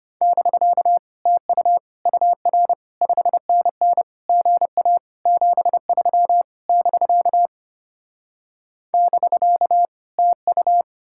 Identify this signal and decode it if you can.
Morse code, heard as a keyed tone, about 33 wpm, 700 Hz. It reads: <BK> TU UR 5NN GA 73 <BK> <BK> TU